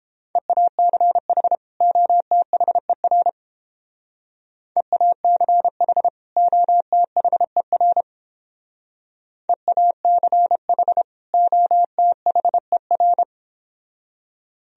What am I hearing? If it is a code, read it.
Morse code, heard as a keyed tone, about 31 wpm, 700 Hz. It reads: EACH OTHER EACH OTHER EACH OTHER